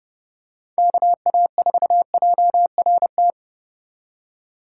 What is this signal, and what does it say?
Morse code, heard as a keyed tone, about 30 wpm, 700 Hz. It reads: KA4JRT